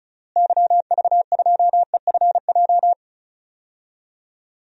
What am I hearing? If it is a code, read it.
Morse code, heard as a keyed tone, about 36 wpm, 700 Hz. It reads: YV2EFJ
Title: DXCC-100 Call Signs - 6 Characters - Repeated 2x - 35wpm